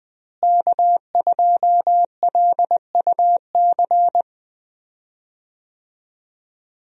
Morse code, heard as a keyed tone, about 20 wpm, 700 Hz. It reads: K2LUC
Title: USA Call Signs - 5 Characters - Repeated 3x - 20wpm